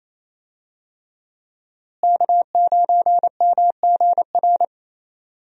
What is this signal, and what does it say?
Morse code, heard as a keyed tone, about 27 wpm, 700 Hz. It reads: K9MGR
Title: USA Call Signs - 5 Characters - Repeated 3x - 28wpm